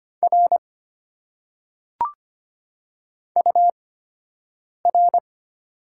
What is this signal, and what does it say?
Morse code, heard as a keyed tone, about 25 wpm, spaced as for 7 wpm, 700 Hz. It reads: R UR
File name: Top 90 QSO Elements - 8x Character Spacing - 25wpm